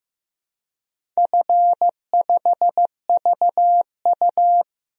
Morse code, heard as a keyed tone, about 15 wpm, 700 Hz. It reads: F5VU
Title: DXCC-100 Call Signs - 4 Characters - Repeated 2x - 15wpm